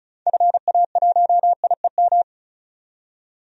Morse code, heard as a keyed tone, about 36 wpm, 700 Hz. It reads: FA1IEM
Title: DXCC-100 Call Signs - 6 Characters - 35wpm